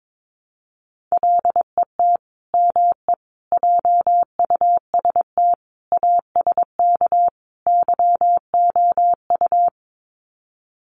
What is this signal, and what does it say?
Morse code, heard as a keyed tone, about 22 wpm, 700 Hz. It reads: LET ME JUST ASK YOU
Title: Sets of 5 Words Encore2 22wpm